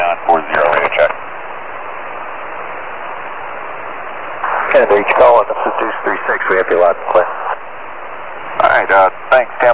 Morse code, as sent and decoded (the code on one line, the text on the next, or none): none